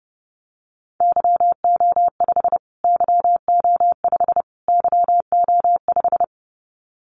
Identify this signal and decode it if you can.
Morse code, heard as a keyed tone, about 30 wpm, 700 Hz. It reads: YO5 YO5 YO5
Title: Callsign Prefixes for Top 25 DXCC - Repeated 3x - 30wpm